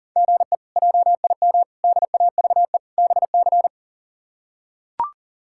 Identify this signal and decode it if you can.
Morse code, heard as a keyed tone, about 40 wpm, 700 Hz. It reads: GE JIM DAVE BC
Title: K1USN SST QSO Practice - 40wpm